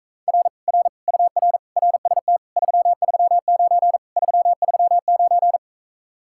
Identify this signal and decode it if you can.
Morse code, heard as a keyed tone, about 45 wpm, 700 Hz. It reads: R R UR RST 339 339